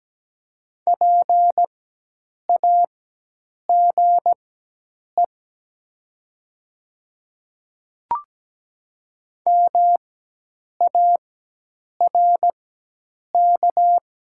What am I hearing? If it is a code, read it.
Morse code, heard as a keyed tone, about 17 wpm, spaced as for 8 wpm, 700 Hz. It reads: PAGE MARK
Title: 4x Character Spacing - Top 300 Words - 17wpm